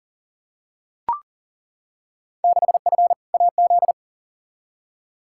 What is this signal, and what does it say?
Morse code, heard as a keyed tone, about 40 wpm, 700 Hz. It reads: 6F AZ
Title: Field Day Exchanges Encore - 40wpm